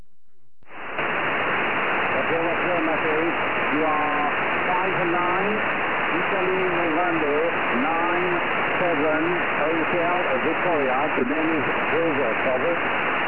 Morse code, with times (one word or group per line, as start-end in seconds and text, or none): none